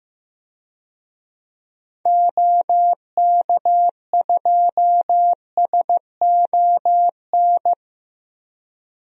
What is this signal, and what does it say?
Morse code, heard as a keyed tone, about 15 wpm, 700 Hz. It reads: OK2SON
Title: DXCC-100 Call Signs - 6 Characters - 15wpm